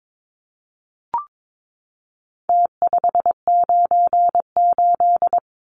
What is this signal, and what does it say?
Morse code, heard as a keyed tone, about 22 wpm, 700 Hz. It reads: T598
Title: Numbers - Contest Style - 4-Digit 22wpm